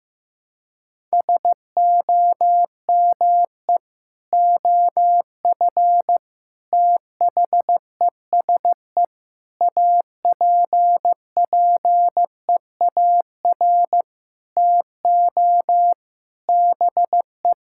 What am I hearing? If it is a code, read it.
Morse code, heard as a keyed tone, about 16 wpm, 700 Hz. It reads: SOME OF THESE APPEAR TO BE
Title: Sentences from Top 1000 Words - Review Part 6 - 15wpm